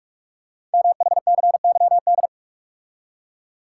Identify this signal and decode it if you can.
Morse code, heard as a keyed tone, about 44 wpm, 700 Hz. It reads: MHCYD
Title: Code Groups - 5 Letters - 45wpm